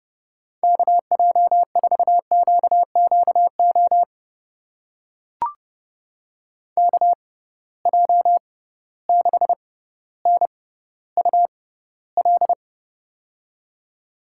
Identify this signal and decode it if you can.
Morse code, heard as a keyed tone, about 31 wpm, spaced as for 16 wpm, 700 Hz. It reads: KJ4QQO KJ6NUL
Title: USA Call Signs - 6 Characters - 6x Character Spacing - 30wpm